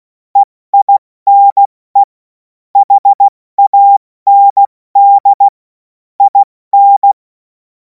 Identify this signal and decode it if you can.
Morse code, heard as a keyed tone, about 16 wpm, 800 Hz. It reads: EINE HAND IN
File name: Morsekurs / Witze / Witz 26. Mai 2026 70 F80